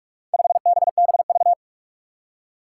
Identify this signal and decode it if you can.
Morse code, heard as a keyed tone, about 41 wpm, 700 Hz. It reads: 5BBV
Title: Four Challenging Letter-Number Combinations 45wpm